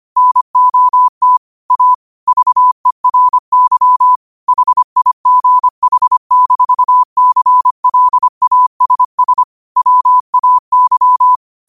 Morse code, heard as a keyed tone, about 25 wpm, 1000 Hz. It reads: NOT A VERY HIGH-CLASS WAY
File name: qotdfiles / 25wpm